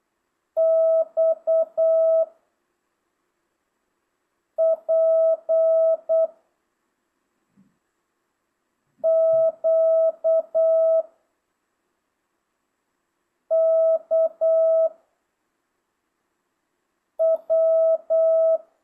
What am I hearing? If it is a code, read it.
Morse code, heard as a keyed tone, about 8 wpm, 660 Hz. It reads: X P Q K W
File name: jh8cbh / 2025cw